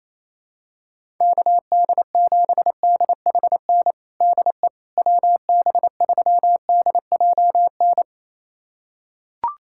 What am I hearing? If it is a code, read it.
Morse code, heard as a keyed tone, about 28 wpm, 700 Hz. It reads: KD7DHN DE WB3DJN
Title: USA Callsign DE Callsign 28wpm